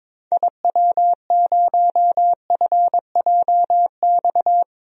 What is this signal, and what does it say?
Morse code, heard as a keyed tone, about 22 wpm, 700 Hz. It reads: IW0FJX
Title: DXCC-100 Call Signs - 6 Characters - 22wpm